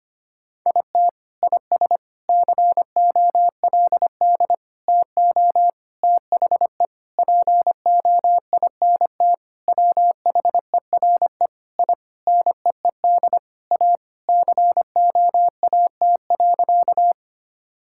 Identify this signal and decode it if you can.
Morse code, heard as a keyed tone, about 25 wpm, 700 Hz. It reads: IT IS COLD TO THE POINT WHERE I NEED A COAT.
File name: practice / 25wpm / Sentences with 4-Word Phrases - 25wpm